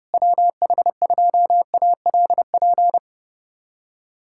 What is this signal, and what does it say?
Morse code, heard as a keyed tone, about 28 wpm, 700 Hz. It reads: WH2ALP
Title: USA Callsign DE Callsign - 6 Characters - 30wpm